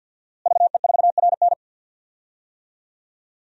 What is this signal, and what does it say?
Morse code, heard as a keyed tone, about 46 wpm, 700 Hz. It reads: VE4RN
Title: DXCC-100 Call Signs - 5 Characters - Repeated 3x - 50wpm